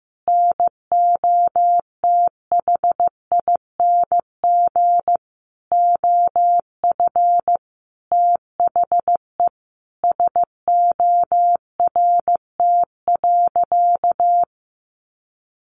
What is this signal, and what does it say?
Morse code, heard as a keyed tone, about 15 wpm, 700 Hz. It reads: NOTHING OF THE SORT.
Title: Sentences from Top 1300 Words 15wpm